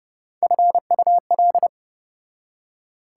Morse code, heard as a keyed tone, about 34 wpm, 700 Hz. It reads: FUL